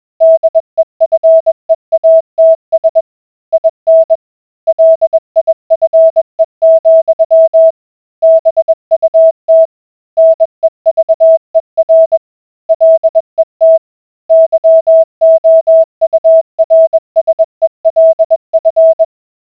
Morse code, heard as a keyed tone, about 21 wpm, 650 Hz. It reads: DEFEATS IN LIFE, BUT NEVER LET YOURSELF